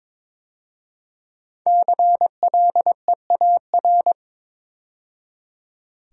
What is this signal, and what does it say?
Morse code, heard as a keyed tone, about 22 wpm, 700 Hz. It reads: CLEAR
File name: practice / 22wpm / 5 Letter Words - First 3 Letters Given - 22wpm